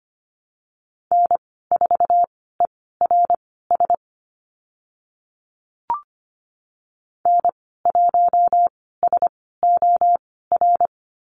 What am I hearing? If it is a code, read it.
Morse code, heard as a keyed tone, about 25 wpm, spaced as for 16 wpm, 690 Hz. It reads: N4ERS N1SOR
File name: Lesson 15g - Number 5 - Call Signs 25wpm